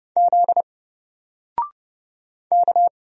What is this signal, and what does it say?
Morse code, heard as a keyed tone, about 30 wpm, 700 Hz. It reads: Z K
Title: Single Character (Letter:Number:Punctuation:Prosign) Alt 30wpm